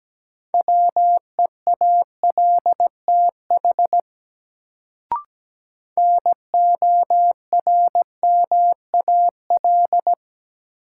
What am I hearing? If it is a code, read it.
Morse code, heard as a keyed tone, about 17 wpm, 700 Hz. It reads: WEALTH NORMAL
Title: Sentences from Top 1200 Words - New Word Drill Alt - 17wpm